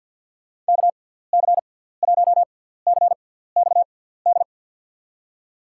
Morse code, heard as a keyed tone, about 52 wpm, spaced as for 17 wpm, 700 Hz. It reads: KC1CXD
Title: USA Call Signs - 6 Characters - 6x Character Spacing - 50wpm